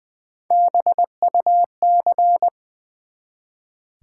Morse code, heard as a keyed tone, about 20 wpm, 700 Hz. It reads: BUC
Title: Build A Word - 1 Letter At A Time - 4-Letter Words - 20wpm